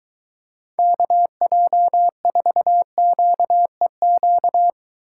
Morse code, heard as a keyed tone, about 23 wpm, 700 Hz. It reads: KJ4QEQ